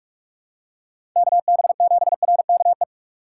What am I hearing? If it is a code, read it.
Morse code, heard as a keyed tone, about 41 wpm, 700 Hz. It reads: KB7RKE
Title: USA Call Signs - 2x Character Spacing - 45wpm